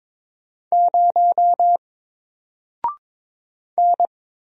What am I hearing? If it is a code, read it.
Morse code, heard as a keyed tone, about 22 wpm, 700 Hz. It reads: 0 N
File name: Single-Letter-Number-22wpm